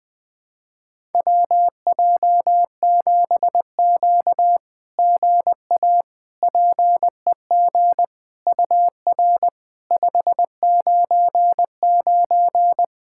Morse code, heard as a keyed tone, about 20 wpm, 690 Hz. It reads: WJ7Q GA PEG UR 599
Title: SOTA QSO Practice 20wpm